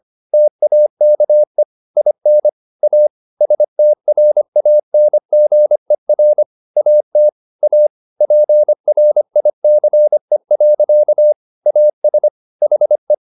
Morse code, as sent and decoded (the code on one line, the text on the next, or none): - .- -.- . / .. -. / .- / ... - .-. .- -. --. .-. / .- - / .- / .--. .-. .. -.-. .-.-.- / .- ... / .... .
TAKE IN A STRANGER AT A PRICE. AS HE